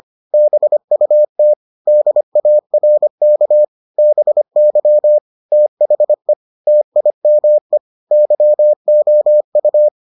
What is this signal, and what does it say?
Morse code, heard as a keyed tone, about 25 wpm, 600 Hz. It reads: BUT DARK BY THE TIME YOU